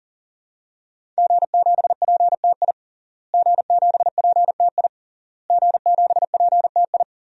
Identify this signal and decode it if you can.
Morse code, heard as a keyed tone, about 40 wpm, 700 Hz. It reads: G7PTI G7PTI G7PTI
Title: DXCC-100 Call Signs - 5 Characters - Repeated 3x - 40wpm